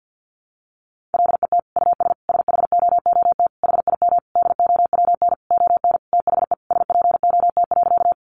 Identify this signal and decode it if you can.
Morse code, heard as a keyed tone, about 50 wpm, 700 Hz. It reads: LET US SHOOT HIM DOWN ON THE SPOT.
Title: Sentences from Top 1200 Words 50wpm